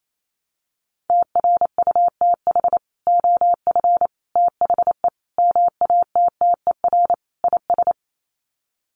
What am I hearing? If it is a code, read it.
Morse code, heard as a keyed tone, about 28 wpm, 700 Hz. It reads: TRUTH OF THE MATTER IS
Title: Sets of 5 Words Encore2 28wpm